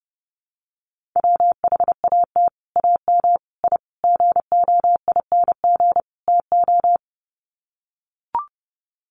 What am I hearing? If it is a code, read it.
Morse code, heard as a keyed tone, about 30 wpm, 700 Hz. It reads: WHAT AM I GOING TO
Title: Sets of 5 Words Encore 30wpm